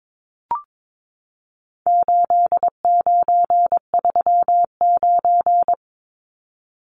Morse code, heard as a keyed tone, about 22 wpm, 700 Hz. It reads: 8939
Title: Numbers - 4-Digit 22wpm